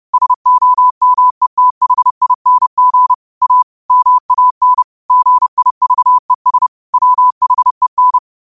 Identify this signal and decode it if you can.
Morse code, heard as a keyed tone, about 30 wpm, 1000 Hz. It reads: SOMETHING A MAN GIVES WHEN
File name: qotdfiles / 30wpm